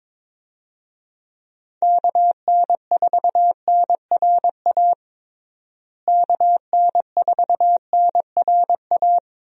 Morse code, heard as a keyed tone, about 22 wpm, 700 Hz. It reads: KN4NRA KN4NRA
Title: USA Call Signs - 6 Characters - Repeated 3x - 22wpm